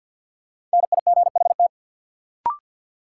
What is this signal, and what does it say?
Morse code, heard as a keyed tone, about 53 wpm, 700 Hz. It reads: NIGHT
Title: Sets of 4 Words - New Word Drill - 50wpm